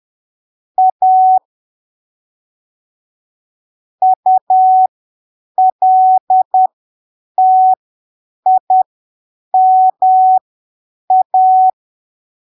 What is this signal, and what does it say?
Morse code, heard as a keyed tone, about 10 wpm, spaced as for 7 wpm, 750 Hz. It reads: A ULTIMA
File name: arquivos / texto / texto 07 10ppm